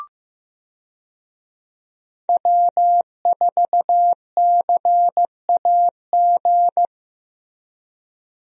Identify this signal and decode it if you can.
Morse code, heard as a keyed tone, about 15 wpm, 700 Hz. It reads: W4CAG
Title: USA Call Signs - 5 Characters - Repeated 2x - 15wpm